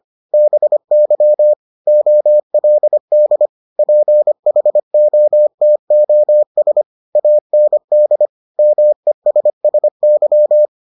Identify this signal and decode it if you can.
Morse code, heard as a keyed tone, about 25 wpm, 600 Hz. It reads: BY OLD PHOTOS AND MESSY